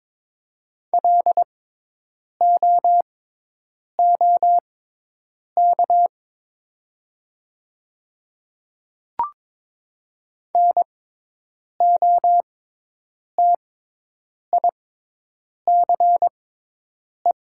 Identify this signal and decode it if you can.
Morse code, heard as a keyed tone, about 22 wpm, spaced as for 8 wpm, 700 Hz. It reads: LOOK NOTICE
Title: 6x Character Spacing - Top 500 Words - 22wpm